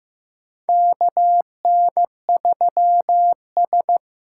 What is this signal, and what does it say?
Morse code, heard as a keyed tone, about 15 wpm, 700 Hz. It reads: KN3S
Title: USA Call Signs - 4 Characters - 3x Character Spacing - 15wpm